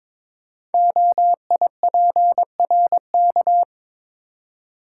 Morse code, heard as a keyed tone, about 22 wpm, 700 Hz. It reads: OIPRK
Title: Code Groups - 5 Letters - 22wpm